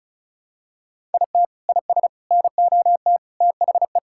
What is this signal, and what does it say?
Morse code, heard as a keyed tone, about 36 wpm, 700 Hz. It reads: IT IS NOT THE
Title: Sentences from Top 1000 Words - Review Part 6 - 35wpm